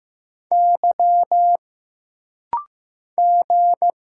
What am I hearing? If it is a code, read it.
Morse code, heard as a keyed tone, about 15 wpm, 700 Hz. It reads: Y G
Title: Single Letter-Number Alt - Rapid-Fire 15wpm